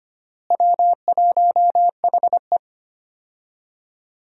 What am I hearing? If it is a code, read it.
Morse code, heard as a keyed tone, about 25 wpm, 700 Hz. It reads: W1HE
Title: USA Call Signs - 4 Characters - Repeated 3x - 25wpm